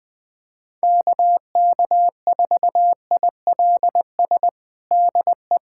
Morse code, heard as a keyed tone, about 21 wpm, 700 Hz. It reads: KK4ILS DE
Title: USA Callsign DE Callsign - 6 Characters - 20wpm